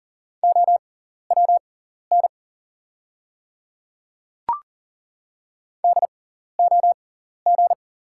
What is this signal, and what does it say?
Morse code, heard as a keyed tone, about 42 wpm, spaced as for 14 wpm, 690 Hz. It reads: OWN DOG
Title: 6x Character Spacing - Top 500 Words - 40wpm